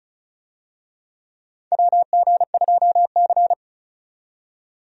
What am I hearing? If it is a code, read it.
Morse code, heard as a keyed tone, about 34 wpm, 700 Hz. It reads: WG2C
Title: USA Call Signs - 4 Characters - 35wpm